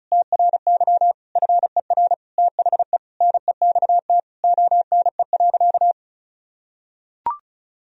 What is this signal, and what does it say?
Morse code, heard as a keyed tone, about 36 wpm, 700 Hz. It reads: TRY FER THE NEXT ONE.
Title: Sentences from Top 400 Words and Abbreviations 35wpm